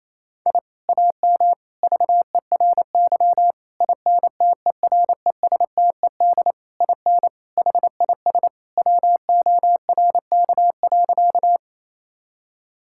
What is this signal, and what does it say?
Morse code, heard as a keyed tone, about 28 wpm, 700 Hz. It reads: I AM VERY INTERESTED IN HIS WORK.
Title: Sentences from Top 1000 Words - Review Part 6 - 28wpm